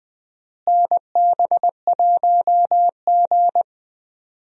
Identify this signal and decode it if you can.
Morse code, heard as a keyed tone, about 21 wpm, 700 Hz. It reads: NB1G
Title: USA Call Signs - 4 Characters - Repeated 3x - 20wpm